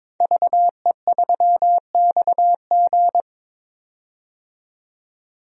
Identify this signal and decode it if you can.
Morse code, heard as a keyed tone, about 21 wpm, 700 Hz. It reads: VE3XG
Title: DXCC-100 Call Signs - 5 Characters - 22wpm